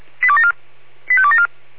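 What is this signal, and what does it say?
Five-tone selective call (ZVEI), heard as quick runs of five tones, tones of about 70 ms. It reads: 85274
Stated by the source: ZVEI five-tone selective call 85274 received by SDR.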